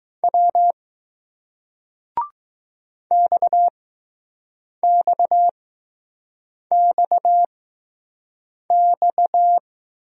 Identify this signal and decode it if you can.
Morse code, heard as a keyed tone, about 20 wpm, spaced as for 6 wpm, 700 Hz. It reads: W XXXX